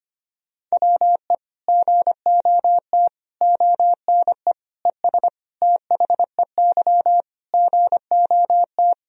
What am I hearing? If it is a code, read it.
Morse code, heard as a keyed tone, about 25 wpm, 700 Hz. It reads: WE GOT ONE ES THEY GOT
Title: Sentences from Top 300 Words and Abbreviations 25wpm